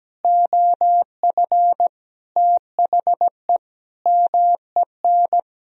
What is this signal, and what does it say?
Morse code, heard as a keyed tone, about 17 wpm, 700 Hz. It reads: OF THE MEN
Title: Sentences from Top 1000 Words - Review Part 2 - 17wpm